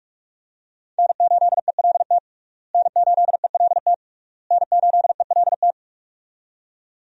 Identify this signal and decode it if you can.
Morse code, heard as a keyed tone, about 46 wpm, 700 Hz. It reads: N8ELT N8ELT N8ELT
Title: USA Call Signs - 5 Characters - Repeated 3x - 45wpm